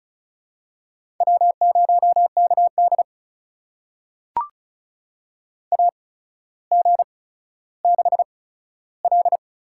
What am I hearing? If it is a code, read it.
Morse code, heard as a keyed tone, about 36 wpm, spaced as for 15 wpm, 700 Hz. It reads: W0KD AG6L
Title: USA Call Signs - 8x Character Spacing - 35wpm